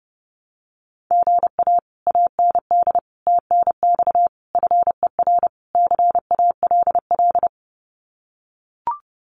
Morse code, heard as a keyed tone, about 30 wpm, 700 Hz. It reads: GA AND TNX FER CALL
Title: Common QSO Phrases Encore 30wpm